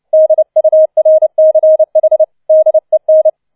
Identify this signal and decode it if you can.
Morse code, heard as a keyed tone, about 29 wpm, 630 Hz. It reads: DURCH DEN
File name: 006 Sechster Streich 30wpm